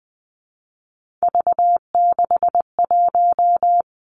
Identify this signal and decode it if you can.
Morse code, heard as a keyed tone, about 20 wpm, 700 Hz. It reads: V61
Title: Three Challenging Letter-Number Combinations 20wpm